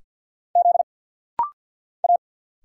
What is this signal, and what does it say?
Morse code, heard as a keyed tone, about 50 wpm, 700 Hz. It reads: Z A
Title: Single Letters - Rapid-Fire 50wpm